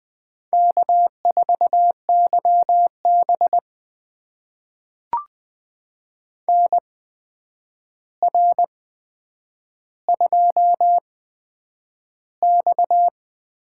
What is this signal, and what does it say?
Morse code, heard as a keyed tone, about 20 wpm, spaced as for 9 wpm, 700 Hz. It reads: K4YB NR2X
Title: USA Call Signs - 4 Characters - 8x Character Spacing - 20wpm